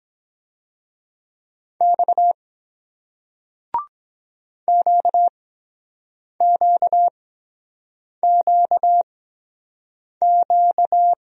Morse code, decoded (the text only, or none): X QQQQ